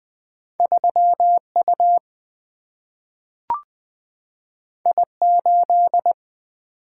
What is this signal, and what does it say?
Morse code, heard as a keyed tone, about 20 wpm, 700 Hz. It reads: 3U I8
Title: Single-Letter Single-Number 20wpm